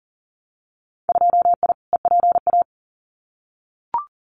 Morse code, heard as a keyed tone, about 42 wpm, 700 Hz. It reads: 2I EPA
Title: Winter Field Day Exchanges Encore 40wpm